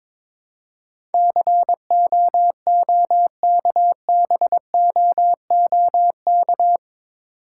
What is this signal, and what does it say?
Morse code, heard as a keyed tone, about 22 wpm, 700 Hz. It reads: COOKBOOK
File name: Difficult Words with Hints - 22wpm